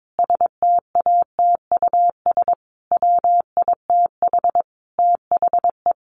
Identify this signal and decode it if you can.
Morse code, heard as a keyed tone, about 22 wpm, 700 Hz. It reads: STATUS WITH THE
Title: Sentences from Top 1400 Words 22wpm